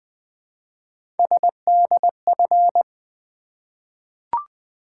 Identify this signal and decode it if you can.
Morse code, heard as a keyed tone, about 20 wpm, 700 Hz. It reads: SDF
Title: Three Letter-Number Combinations - 20wpm